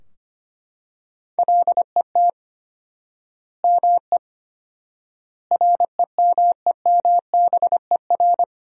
Morse code, heard as a keyed tone, about 25 wpm, 700 Hz. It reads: LET ME REMEMBER
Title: 4x Word-Spacing - Sentences from Top 400 Words - 25wpm